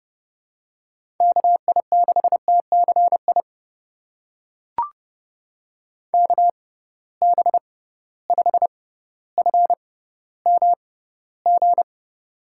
Morse code, heard as a keyed tone, about 30 wpm, spaced as for 15 wpm, 700 Hz. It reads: KI6TCI KB5FMG
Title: USA Call Signs - 6x Character Spacing - 30wpm